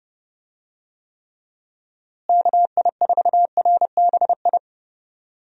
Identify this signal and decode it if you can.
Morse code, heard as a keyed tone, about 31 wpm, 700 Hz. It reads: KI4RBI
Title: USA Call Signs - 6 Characters - Repeated 2x - 30wpm